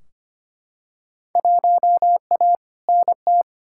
Morse code, heard as a keyed tone, about 25 wpm, 700 Hz. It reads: 1A NT
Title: Field Day Exchanges Encore - 25wpm